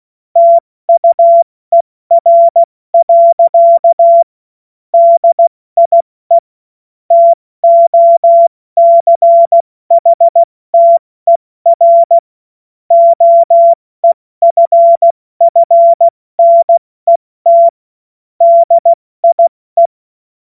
Morse code, heard as a keyed tone, about 16 wpm, 680 Hz. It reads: TUER. DIE TOCHTER OEFFNET DIE
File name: Morsekurs / Witze / Witz 25. April 70 F80